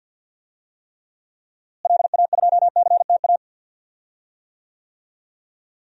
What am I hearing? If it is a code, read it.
Morse code, heard as a keyed tone, about 48 wpm, 700 Hz. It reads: LA2CTA
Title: DXCC-100 Call Signs - 6 Characters - Repeated 2x - 50wpm